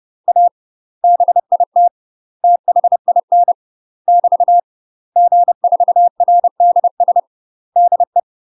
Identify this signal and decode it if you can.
Morse code, heard as a keyed tone, about 30 wpm, 700 Hz. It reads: A BIT THIN <BT> G4RDS DE
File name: UK MASTER QSOs TRAINING / 30WPM / UK MASTER QSOs 0023